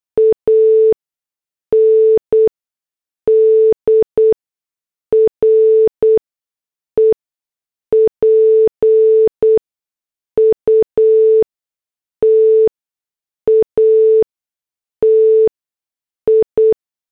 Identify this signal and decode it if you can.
Morse code, heard as a keyed tone, about 8 wpm, spaced as for 6 wpm, 440 Hz. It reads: AND REPUTATI